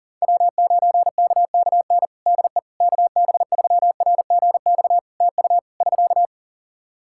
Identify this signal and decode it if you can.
Morse code, heard as a keyed tone, about 37 wpm, 700 Hz. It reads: W9KKN DE KB3RGX TU <SK>
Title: Practice QSO - Exchange 4 - Ending - 40wpm